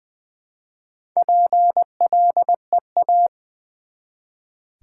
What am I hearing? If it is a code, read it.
Morse code, heard as a keyed tone, about 20 wpm, 700 Hz. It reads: PLEA
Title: Build A Word - 1 Letter At A Time - 4-Letter Words - 20wpm